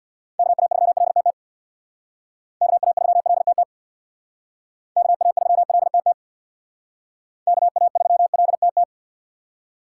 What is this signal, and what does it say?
Morse code, heard as a keyed tone, about 64 wpm, 700 Hz. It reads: KA3LTT KA3LTT KA3LTT KA3LTT